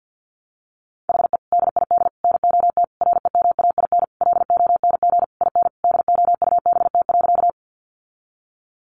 Morse code, heard as a keyed tone, about 50 wpm, 700 Hz. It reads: HE DID NOT REMAIN LONG IN DOUBT.
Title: Sentences from Top 1500 Words 50wpm